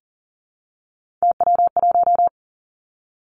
Morse code, heard as a keyed tone, about 40 wpm, 700 Hz. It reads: TW1
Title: Three Challenging Letter-Number Combinations 40wpm